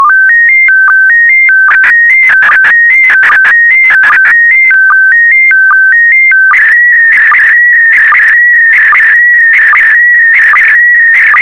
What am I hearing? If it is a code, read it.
SSTV (slow-scan television), PD 160.
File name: sstv / PD160-VK4MSL